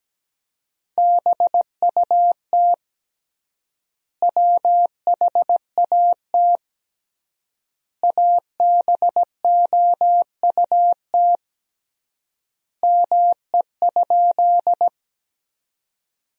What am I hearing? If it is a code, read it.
Morse code, heard as a keyed tone, about 17 wpm, 700 Hz. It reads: BUT WHAT ABOUT ME?
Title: 3x Word-Spacing - Sentences from Top 200 Words - 17wpm